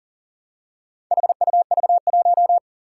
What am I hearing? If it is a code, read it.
Morse code, heard as a keyed tone, about 41 wpm, 700 Hz. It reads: HUV1
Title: Four Challenging Letter-Number Combinations 40wpm